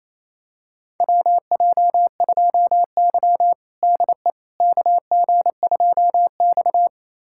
Morse code, heard as a keyed tone, about 28 wpm, 700 Hz. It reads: WJ2Y DE KG2X
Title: USA Callsign DE Callsign - 4 Characters - 28wpm